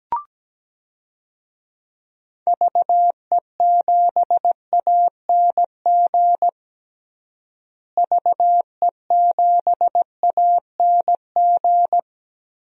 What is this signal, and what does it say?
Morse code, heard as a keyed tone, about 17 wpm, 700 Hz. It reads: VE7ANG VE7ANG
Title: DXCC-100 Call Signs - 6 Characters - Repeated 3x - 17wpm